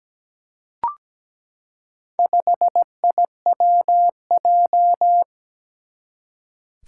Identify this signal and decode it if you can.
Morse code, heard as a keyed tone, about 17 wpm, 700 Hz. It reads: 5IWJ
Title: Four Challenging Letter-Number Combinations 17wpm